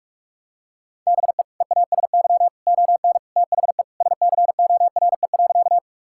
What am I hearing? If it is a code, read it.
Morse code, heard as a keyed tone, about 46 wpm, 700 Hz. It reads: BE EASY ON THE SCORE.
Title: Sentences from Top 1600 Words 45wpm